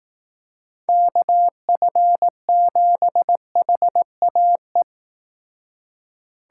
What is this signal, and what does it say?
Morse code, heard as a keyed tone, about 18 wpm, 700 Hz. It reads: KF7HAE
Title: Speed Racing - USA Call Signs 15wpm